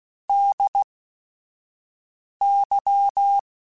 Morse code, heard as a keyed tone, about 16 wpm, spaced as for 5 wpm, 800 Hz. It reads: DY
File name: Technical Subjects / Technical Subjects-mp3s / AND Gate @5